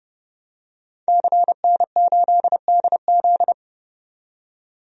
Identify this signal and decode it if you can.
Morse code, heard as a keyed tone, about 30 wpm, 700 Hz. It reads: CN8DZ
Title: DXCC-100 Call Signs - 5 Characters - 30wpm